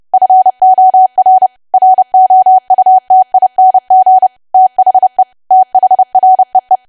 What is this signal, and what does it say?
Morse code, heard as a keyed tone, about 30 wpm, 750 Hz. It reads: FOR ROUTING THE THREE
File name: cwa / W1AW-Level3 / 30WPM